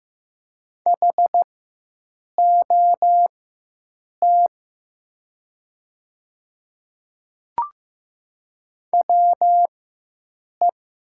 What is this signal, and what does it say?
Morse code, heard as a keyed tone, about 15 wpm, spaced as for 7 wpm, 700 Hz. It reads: HOT WE
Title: 4x Character Spacing - Top 100 Words - 15wpm